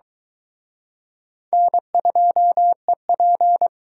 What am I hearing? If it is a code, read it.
Morse code, heard as a keyed tone, about 23 wpm, 700 Hz. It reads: N2EP